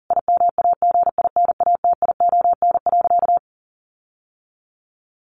Morse code, heard as a keyed tone, about 40 wpm, 700 Hz. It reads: IMAGINATION.
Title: Sentences from Top 1300 Words 40wpm